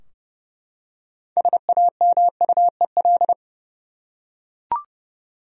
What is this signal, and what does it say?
Morse code, heard as a keyed tone, about 30 wpm, 700 Hz. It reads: SAMUEL
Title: USA Common Names - 30wpm